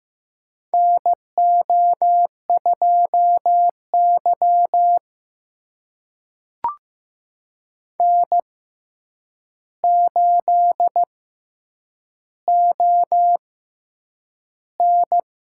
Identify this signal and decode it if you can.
Morse code, heard as a keyed tone, about 15 wpm, spaced as for 8 wpm, 700 Hz. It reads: NO2Y N8ON
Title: USA Call Signs - 6x Character Spacing - 15wpm